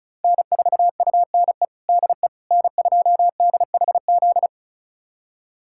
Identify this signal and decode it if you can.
Morse code, heard as a keyed tone, about 36 wpm, 700 Hz. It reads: N4UNE DE N2DHZ
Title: USA Callsign DE Callsign - 5 Characters - 35wpm